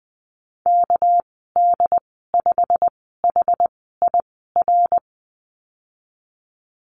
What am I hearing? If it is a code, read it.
Morse code, heard as a keyed tone, about 20 wpm, spaced as for 15 wpm, 700 Hz. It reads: KD5HIR
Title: 219 Lesson 35g - Number 0 - Call Signs 20wpm